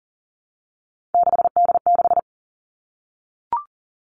Morse code, heard as a keyed tone, about 41 wpm, 700 Hz. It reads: 6D6
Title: Three Challenging Letter-Number Combinations 40wpm